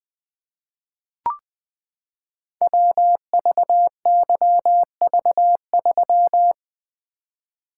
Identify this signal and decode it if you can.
Morse code, heard as a keyed tone, about 20 wpm, 700 Hz. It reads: WVYV3
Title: Code Groups - 5 Letter:Numbers - 20wpm